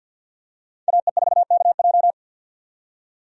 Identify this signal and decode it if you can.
Morse code, heard as a keyed tone, about 51 wpm, 700 Hz. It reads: AE4KJ
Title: USA Call Signs - 5 Characters - 50wpm